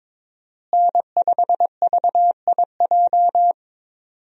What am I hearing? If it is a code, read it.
Morse code, heard as a keyed tone, about 22 wpm, 700 Hz. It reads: N5VIJ
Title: USA Call Signs - 5 Characters - Repeated 2x - 22wpm